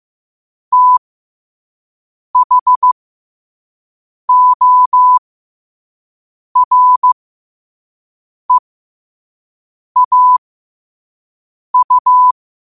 Morse code, heard as a keyed tone, about 15 wpm, spaced as for 5 wpm, 1000 Hz. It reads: THOREAU
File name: qotdfiles / 5wpm